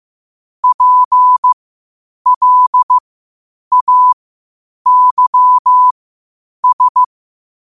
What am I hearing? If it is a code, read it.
Morse code, heard as a keyed tone, about 15 wpm, spaced as for 8 wpm, 1000 Hz. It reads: PLAYS
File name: qotdfiles / 8wpm